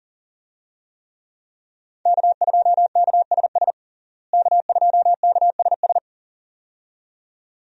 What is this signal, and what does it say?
Morse code, heard as a keyed tone, about 37 wpm, 700 Hz. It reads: K2KSS K2KSS
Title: USA Call Signs - 5 Characters - Repeated 2x - 40wpm